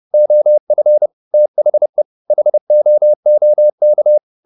Morse code, heard as a keyed tone, about 30 wpm, 600 Hz. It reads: OF THE HOOK
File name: the old man and the sea / the old man and the sea 1 30wpm